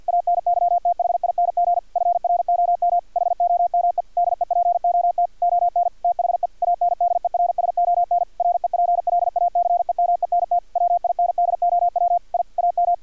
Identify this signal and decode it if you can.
Morse code, heard as a keyed tone, about 50 wpm, 700 Hz, with radio noise. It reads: ANYTHING FROM HOME DEPOT ON THE ANDERSON REPLACEMENT WINDOW I AM
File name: archived / CRSnet